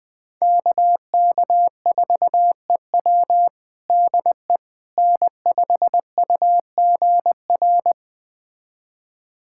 Morse code, heard as a keyed tone, about 21 wpm, 700 Hz. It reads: KK4EW DE N5UGR
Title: USA Callsign DE Callsign - 5 Characters - 20wpm